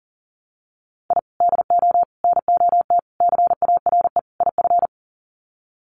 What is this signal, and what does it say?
Morse code, heard as a keyed tone, about 40 wpm, 700 Hz. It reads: I DO NOT CARE IF
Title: Sets of 5 Words 40wpm